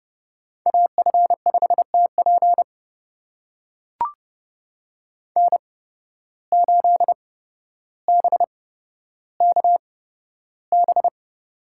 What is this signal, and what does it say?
Morse code, heard as a keyed tone, about 31 wpm, spaced as for 13 wpm, 700 Hz. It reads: AF5TP N8BKB
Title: USA Call Signs - 5 Characters - 8x Character Spacing - 30wpm